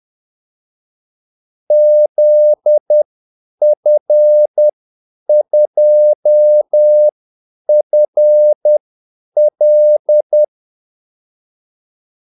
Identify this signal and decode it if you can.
Morse code, heard as a keyed tone, about 10 wpm, spaced as for 8 wpm, 600 Hz. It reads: ZF2FL
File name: Callsigns-10-8wpm